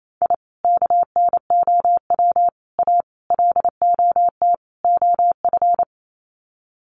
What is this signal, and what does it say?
Morse code, heard as a keyed tone, about 28 wpm, 700 Hz. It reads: I KNOW A LOT OF
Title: Sets of 5 Words Encore 28wpm